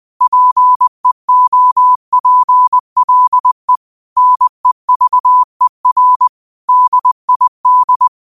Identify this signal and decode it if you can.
Morse code, heard as a keyed tone, about 20 wpm, 1000 Hz. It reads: PEOPLE NEVER DID
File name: qotdfiles / 20wpm